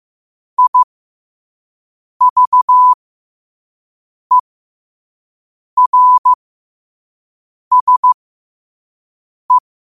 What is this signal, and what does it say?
Morse code, heard as a keyed tone, about 15 wpm, spaced as for 5 wpm, 1000 Hz. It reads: IVERSE
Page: New quotes every day in morse code at 5 Words per minute.